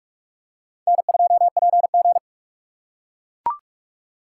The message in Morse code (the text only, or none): N2PG